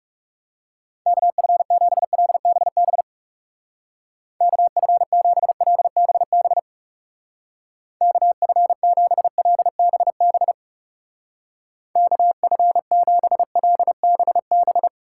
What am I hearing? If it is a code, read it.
Morse code, heard as a keyed tone, about 37 wpm, 700 Hz. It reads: KF7LBB KF7LBB KF7LBB KF7LBB
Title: Speed Racing - USA 2x3 Call Signs 30wpm